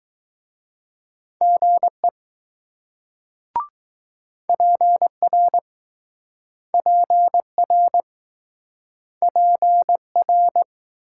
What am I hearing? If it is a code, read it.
Morse code, heard as a keyed tone, about 20 wpm, 700 Hz. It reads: GE PR PR PR